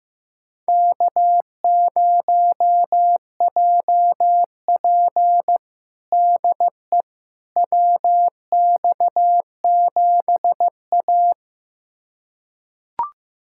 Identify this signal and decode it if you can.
Morse code, heard as a keyed tone, about 15 wpm, 700 Hz. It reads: K0JP DE WX7A
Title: USA Callsign DE Callsign - 4 Characters - 15wpm